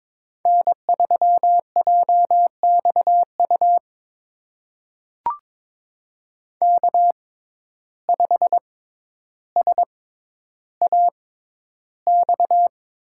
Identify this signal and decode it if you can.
Morse code, heard as a keyed tone, about 22 wpm, spaced as for 11 wpm, 700 Hz. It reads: N3JXU K5SAX
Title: USA Call Signs - 5 Characters - 6x Character Spacing - 22wpm